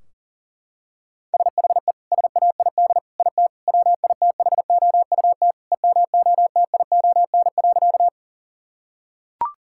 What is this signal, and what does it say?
Morse code, heard as a keyed tone, about 40 wpm, 700 Hz. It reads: SHE SAID IT WITHOUT EMOTION.
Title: Sentences from Top 1600 Words 40wpm